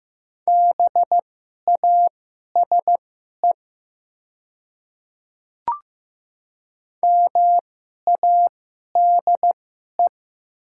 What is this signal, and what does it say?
Morse code, heard as a keyed tone, about 15 wpm, spaced as for 11 wpm, 690 Hz. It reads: BASE MADE
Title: 2x Character Spacing - Top 300 Words - 15wpm